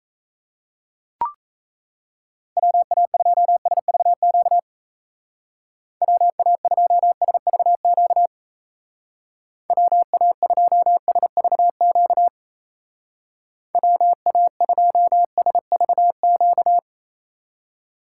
Morse code, heard as a keyed tone, about 34 wpm, 700 Hz. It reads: WA2SVQ WA2SVQ WA2SVQ WA2SVQ
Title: Speed Racing - USA 2x3 Call Signs 28wpm